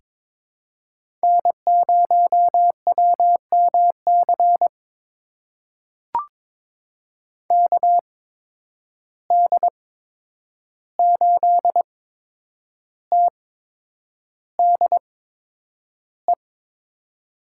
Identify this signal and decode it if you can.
Morse code, heard as a keyed tone, about 22 wpm, spaced as for 9 wpm, 700 Hz. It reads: N0WMC KD8TDE
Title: USA Call Signs - 8x Character Spacing - 22wpm